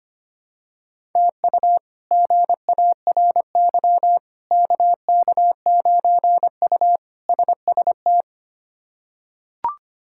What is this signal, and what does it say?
Morse code, heard as a keyed tone, about 25 wpm, 700 Hz. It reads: TU GARY KK9U SST
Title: K1USN SST QSO Practice - 25wpm